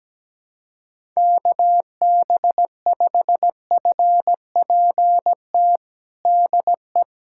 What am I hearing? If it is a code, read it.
Morse code, heard as a keyed tone, about 17 wpm, 700 Hz. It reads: KB5FPT DE
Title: USA Callsign DE Callsign - 6 Characters - 17wpm